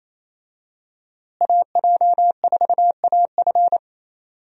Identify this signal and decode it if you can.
Morse code, heard as a keyed tone, about 28 wpm, 700 Hz. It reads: AJ4AF
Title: USA Call Signs - 5 Characters - Repeated 3x - 28wpm